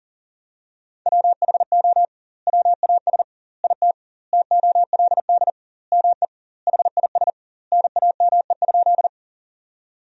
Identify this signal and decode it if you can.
Morse code, heard as a keyed tone, about 40 wpm, 700 Hz. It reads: WHO WAS IT TOLD ME HIS NAME?
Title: 2x Word-Spacing - Sentences from Top 400 Words - 40wpm